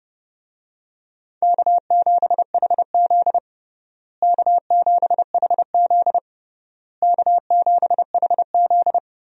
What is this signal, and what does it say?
Morse code, heard as a keyed tone, about 29 wpm, 700 Hz. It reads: K7HZ K7HZ K7HZ
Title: USA Call Signs - 4 Characters - Repeated 3x - 30wpm